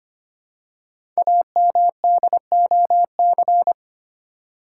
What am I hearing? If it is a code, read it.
Morse code, heard as a keyed tone, about 25 wpm, 700 Hz. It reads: AMDOC
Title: Code Groups - 5 Letters - 25wpm